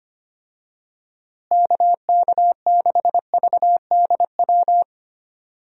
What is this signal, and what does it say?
Morse code, heard as a keyed tone, about 25 wpm, 700 Hz. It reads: KK6VDW
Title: USA Call Signs - 6 Characters - Repeated 3x - 25wpm